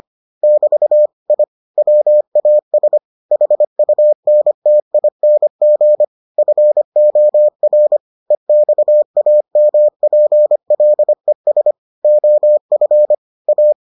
Morse code, heard as a keyed tone, about 25 wpm, 600 Hz. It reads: <BT> I WAS HUNTING FOR EXAMPLES OF A